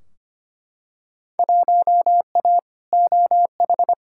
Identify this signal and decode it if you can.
Morse code, heard as a keyed tone, about 25 wpm, 700 Hz. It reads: 1A OH
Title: Field Day Exchanges - 25wpm